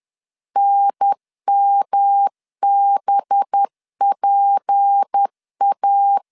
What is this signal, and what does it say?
Morse code, heard as a keyed tone, about 10 wpm, 800 Hz. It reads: NMBPA
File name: LEZIONI / 7 GF9Z / 7 GF9 10wpm 2747